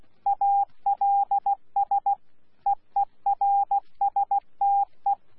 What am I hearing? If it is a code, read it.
Morse code, heard as a keyed tone, about 16 wpm, 800 Hz. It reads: ALS EERSTE
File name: Z PI4VRZ / PI4VRZ 14-jun-2025